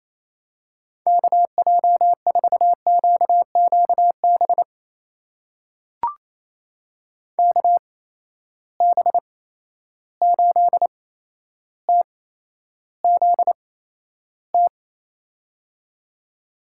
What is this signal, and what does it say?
Morse code, heard as a keyed tone, about 28 wpm, spaced as for 12 wpm, 700 Hz. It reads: KJ4QQB KB8TZT
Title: USA Call Signs - 6 Characters - 8x Character Spacing - 28wpm